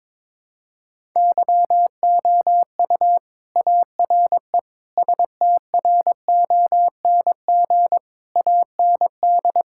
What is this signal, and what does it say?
Morse code, heard as a keyed tone, about 22 wpm, 700 Hz. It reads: YOU ARE STRONG AND
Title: Sentences from Top 1000 Words - Review Part 5 - 22wpm